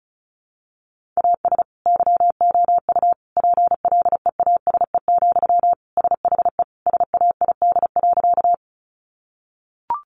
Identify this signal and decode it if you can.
Morse code, heard as a keyed tone, about 34 wpm, 700 Hz. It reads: AS YOU PLEASE, SHE SAID.
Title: Sentences from Top 1500 Words 35wpm